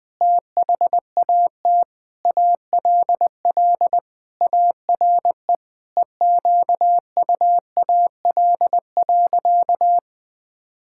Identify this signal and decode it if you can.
Morse code, heard as a keyed tone, about 20 wpm, 700 Hz. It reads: THAT ALL ARE EQUAL.
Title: Sentences with 9-Word Phrases - 20wpm